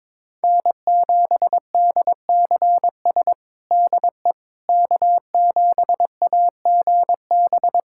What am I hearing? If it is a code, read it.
Morse code, heard as a keyed tone, about 22 wpm, 700 Hz. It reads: N7DCS DE K7AGB
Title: USA Callsign DE Callsign 22wpm